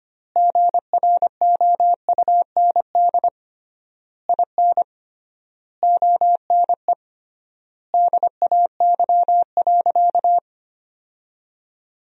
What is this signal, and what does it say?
Morse code, heard as a keyed tone, about 25 wpm, 700 Hz. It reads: GROUND IN ONE DAY.
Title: 3x Word-Spacing - Sentences from Top 500 Words - 25wpm